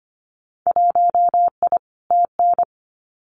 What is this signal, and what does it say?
Morse code, heard as a keyed tone, about 25 wpm, 700 Hz. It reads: 1I TN
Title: Winter Field Day Exchanges 25wpm